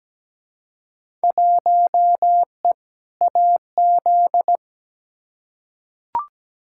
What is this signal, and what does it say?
Morse code, heard as a keyed tone, about 17 wpm, 700 Hz. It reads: 1E AZ
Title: Field Day Exchanges - 17wpm